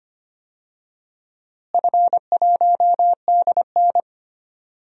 Morse code, heard as a keyed tone, about 25 wpm, 700 Hz. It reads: F1DN
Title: DXCC-100 Call Signs - 4 Characters - Repeated 2x - 25wpm